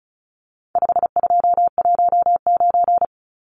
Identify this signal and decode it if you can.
Morse code, heard as a keyed tone, about 34 wpm, 700 Hz. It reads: 5219
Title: Numbers - 4-Digit 35wpm